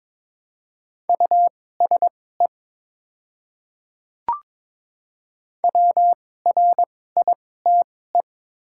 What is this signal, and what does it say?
Morse code, heard as a keyed tone, about 22 wpm, spaced as for 16 wpm, 700 Hz. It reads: USE WRITE
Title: 2x Character Spacing - Top 100 Words - 22wpm